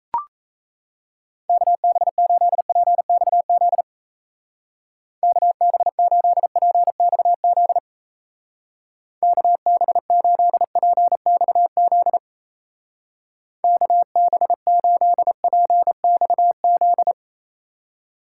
Morse code, heard as a keyed tone, about 35 wpm, 700 Hz. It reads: KB8PXZ KB8PXZ KB8PXZ KB8PXZ
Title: Speed Racing - USA 2x3 Call Signs 28wpm